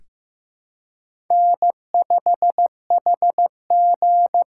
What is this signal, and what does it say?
Morse code, heard as a keyed tone, about 15 wpm, 700 Hz. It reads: N5HG
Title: USA Call Signs - 5 Characters - 8x Character Spacing - 15wpm